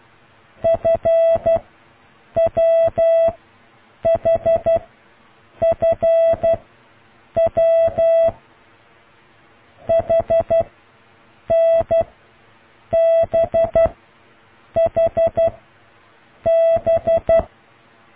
Morse code, heard as a keyed tone, about 12 wpm, 660 Hz. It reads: F W H F W H N B H B